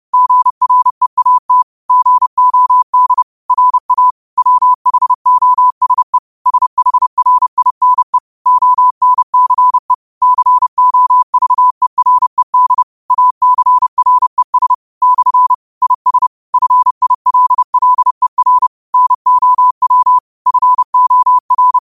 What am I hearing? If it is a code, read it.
Morse code, heard as a keyed tone, about 30 wpm, 1000 Hz. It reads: GREAT GOD RA WHOSE SHRINE ONCE COVERED ACRES / IS FILLER NOW FOR
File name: qotdfiles / 30wpm